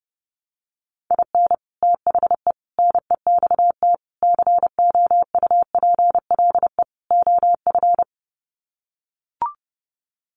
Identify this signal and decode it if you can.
Morse code, heard as a keyed tone, about 30 wpm, 700 Hz. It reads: IN THE NEXT COUPLE OF
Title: Sets of 5 Words Encore2 30wpm